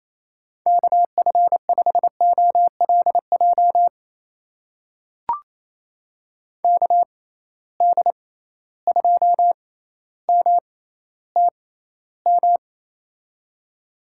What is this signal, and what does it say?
Morse code, heard as a keyed tone, about 28 wpm, spaced as for 14 wpm, 700 Hz. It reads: KF5OLJ KD2MTM
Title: USA Call Signs - 6 Characters - 6x Character Spacing - 28wpm